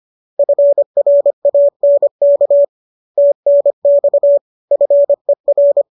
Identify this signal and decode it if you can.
Morse code, heard as a keyed tone, about 25 wpm, 570 Hz. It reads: FRANK TNX FER
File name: qso221 25